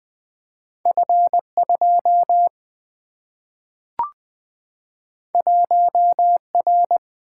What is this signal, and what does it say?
Morse code, heard as a keyed tone, about 20 wpm, 700 Hz. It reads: F2 1R
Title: Single-Letter Single-Number 20wpm